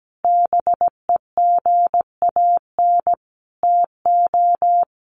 Morse code, heard as a keyed tone, about 17 wpm, 700 Hz. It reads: BEGAN TO
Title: Sentences from Top 900 Words 17wpm